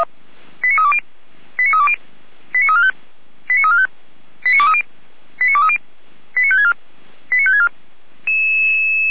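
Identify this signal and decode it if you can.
Five-tone selective call (ZVEI), heard as quick runs of five tones, tones of about 70 ms. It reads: 89320 89320 89345 89345 89230 89230 89564 89564
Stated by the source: ZVEI five-tone call 89010 received by SDR, Feuerwehr Orlamünde.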